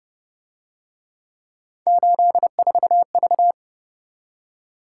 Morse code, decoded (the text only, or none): 84V